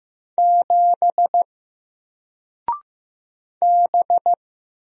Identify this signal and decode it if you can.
Morse code, heard as a keyed tone, about 15 wpm, 700 Hz. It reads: 7 B
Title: Single Character (Letter:Number:Punctuation:Prosign) 15wpm